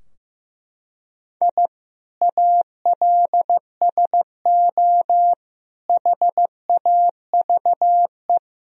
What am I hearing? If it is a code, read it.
Morse code, heard as a keyed tone, about 16 wpm, 700 Hz. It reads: I ALSO HAVE
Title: Sentences from Top 1000 Words - Review Part 3 - 15wpm